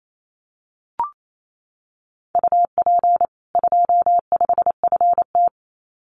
Morse code, heard as a keyed tone, about 28 wpm, 700 Hz. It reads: UP 25FT
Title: Common QSO Phrases 28wpm